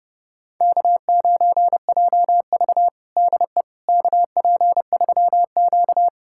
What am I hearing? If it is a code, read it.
Morse code, heard as a keyed tone, about 30 wpm, 700 Hz. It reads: K9JV DE KP3Q
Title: USA Callsign DE Callsign - 4 Characters - 30wpm